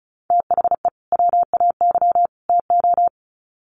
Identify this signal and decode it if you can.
Morse code, heard as a keyed tone, about 34 wpm, 700 Hz. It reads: THE WAY TO
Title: Sets of 3 Words 35wpm